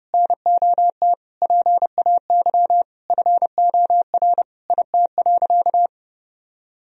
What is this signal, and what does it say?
Morse code, heard as a keyed tone, about 30 wpm, 700 Hz. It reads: NOT PAY FOR IT.
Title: Sentences from Top 1000 Words - Review Part 3 - 30wpm